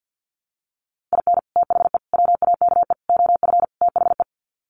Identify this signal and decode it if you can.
Morse code, heard as a keyed tone, about 50 wpm, 700 Hz. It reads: IN THE WAKE OF THE
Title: Sets of 5 Words 50wpm